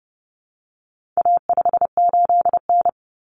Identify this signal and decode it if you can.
Morse code, heard as a keyed tone, about 30 wpm, 700 Hz. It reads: A58N
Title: Common Cut Numbers - 4-Digit 30wpm